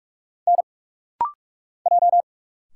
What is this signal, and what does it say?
Morse code, heard as a keyed tone, about 44 wpm, 700 Hz. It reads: N J
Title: Single Letters - Rapid-Fire 45wpm